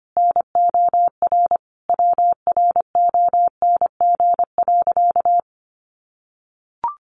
Morse code, heard as a keyed tone, about 25 wpm, 700 Hz. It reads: NOR WRONG.
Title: Sentences from Top 1500 Words 25wpm